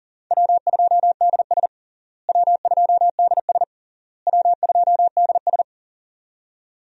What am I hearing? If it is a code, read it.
Morse code, heard as a keyed tone, about 40 wpm, 700 Hz. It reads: W2DS W2DS W2DS
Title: USA Call Signs - 4 Characters - Repeated 3x - 40wpm